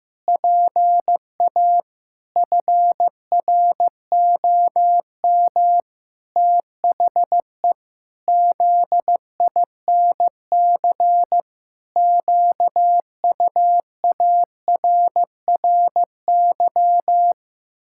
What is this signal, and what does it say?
Morse code, heard as a keyed tone, about 14 wpm, 700 Hz. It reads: PA FROM THE ZINC QUARRY
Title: Pangram Challenge v2 - 15wpm